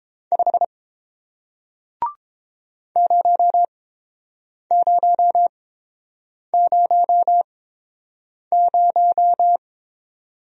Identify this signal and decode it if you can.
Morse code, heard as a keyed tone, about 30 wpm, spaced as for 7 wpm, 690 Hz. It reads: 5 0000